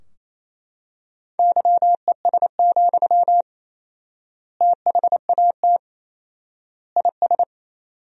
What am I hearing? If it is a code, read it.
Morse code, heard as a keyed tone, about 28 wpm, 700 Hz. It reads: YES, THAT IS
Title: 4x Word-Spacing - Sentences from Top 500 Words - 28wpm